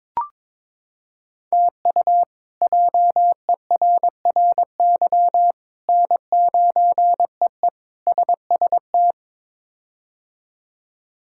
Morse code, heard as a keyed tone, about 22 wpm, 700 Hz. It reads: TU JERRY N9EE SST
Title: K1USN SST QSO Practice - 22wpm